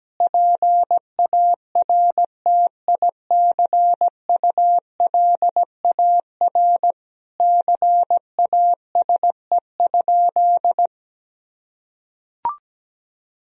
Morse code, heard as a keyed tone, about 17 wpm, 700 Hz. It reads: PARTICULAR CASE?
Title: Sentences from Top 1000 Words - Review Part 4 - 17wpm